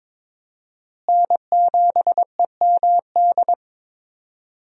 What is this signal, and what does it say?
Morse code, heard as a keyed tone, about 22 wpm, 700 Hz. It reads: N7EMD
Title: USA Call Signs - 5 Characters - Repeated 3x - 22wpm